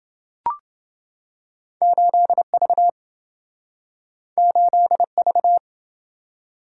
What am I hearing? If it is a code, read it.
Morse code, heard as a keyed tone, about 27 wpm, 700 Hz. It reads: 8V 8V